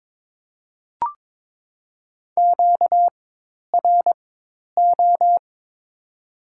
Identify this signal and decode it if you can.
Morse code, heard as a keyed tone, about 22 wpm, spaced as for 10 wpm, 700 Hz. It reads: QRO
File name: Top 90 QSO Elements - 4x Character Spacing - 22wpm